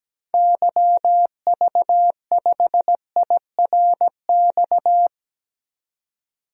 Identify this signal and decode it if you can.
Morse code, heard as a keyed tone, about 17 wpm, 700 Hz. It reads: YV5IRX
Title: DXCC-100 Call Signs - 6 Characters - 17wpm